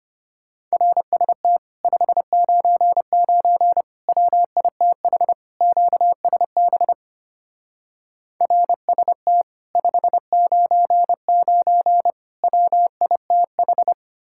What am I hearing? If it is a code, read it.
Morse code, heard as a keyed tone, about 31 wpm, 700 Hz. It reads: RST 599 WITH QSB RST 599 WITH